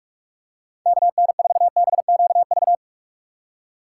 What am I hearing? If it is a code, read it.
Morse code, heard as a keyed tone, about 43 wpm, 700 Hz. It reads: KN4BQV
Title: USA Call Signs - 2x Character Spacing - 45wpm